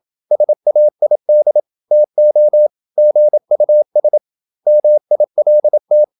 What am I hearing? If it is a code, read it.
Morse code, heard as a keyed tone, about 27 wpm, 600 Hz. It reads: SAID TO GUS MILT
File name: Never 0027wpm